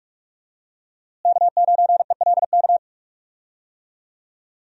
Morse code, heard as a keyed tone, about 42 wpm, 700 Hz. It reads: K9ELK
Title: USA Call Signs - 5 Characters - Repeated 2x - 45wpm